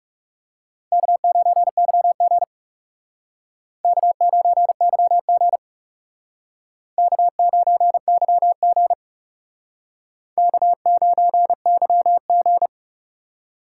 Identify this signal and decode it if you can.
Morse code, heard as a keyed tone, about 38 wpm, 700 Hz. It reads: K9YG K9YG K9YG K9YG